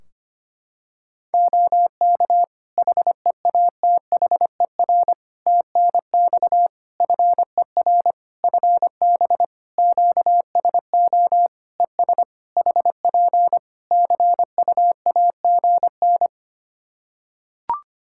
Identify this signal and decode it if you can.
Morse code, heard as a keyed tone, about 25 wpm, 700 Hz. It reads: OK HEATHER TNX FER FB QSO ES HP CUAGN
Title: K1USN SST QSO Practice 25wpm